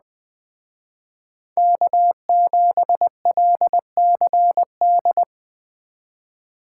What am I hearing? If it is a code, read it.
Morse code, heard as a keyed tone, about 20 wpm, 700 Hz. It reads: K7LCD
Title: Speed Racing - USA 1x3 Call Signs 17wpm